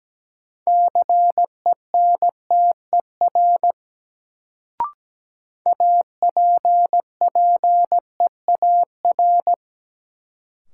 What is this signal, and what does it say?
Morse code, heard as a keyed tone, about 17 wpm, 700 Hz. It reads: CENTER APPEAR
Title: Sentences from Top 1000 Words - Review Part 2 - New Word Drill - 17wpm